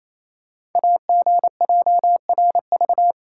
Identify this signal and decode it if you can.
Morse code, heard as a keyed tone, about 27 wpm, 700 Hz. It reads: AGJRV
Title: Code Groups - 5 Letter:Numbers - 28wpm